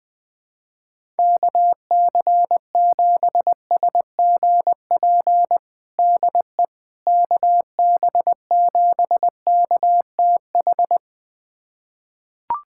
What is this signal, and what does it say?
Morse code, heard as a keyed tone, about 20 wpm, 700 Hz. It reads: KC7SGP DE KB7KTH
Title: USA Callsign DE Callsign 20wpm